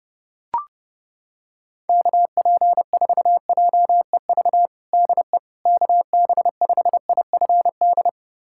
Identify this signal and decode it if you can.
Morse code, heard as a keyed tone, about 30 wpm, 700 Hz. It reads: KP4JEV DE KB5IFD
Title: USA Callsign DE Callsign 30wpm